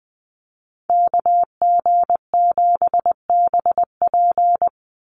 Morse code, heard as a keyed tone, about 20 wpm, 700 Hz. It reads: KG7BP
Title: USA Call Signs 20wpm